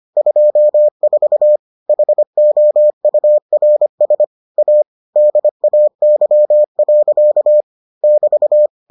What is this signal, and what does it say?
Morse code, heard as a keyed tone, about 25 wpm, 600 Hz. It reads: 24 HOURS A DAY. <BT>